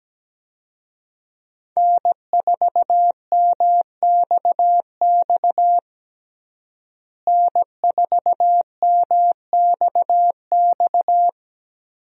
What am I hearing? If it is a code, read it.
Morse code, heard as a keyed tone, about 17 wpm, 700 Hz. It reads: N4MXX N4MXX
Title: USA Call Signs - 5 Characters - Repeated 3x - 17wpm